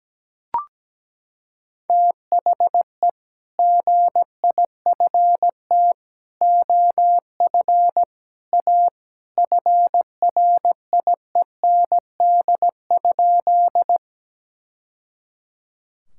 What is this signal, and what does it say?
Morse code, heard as a keyed tone, about 17 wpm, 700 Hz. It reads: THE GIFT OF A FRIEND?
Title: Sentences from Top 1000 Words - Review Part 6 - 17wpm